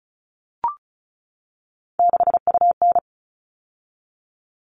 Morse code, heard as a keyed tone, about 34 wpm, 700 Hz. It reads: 6UN
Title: All Cut Numbers - 3-Digit 35wpm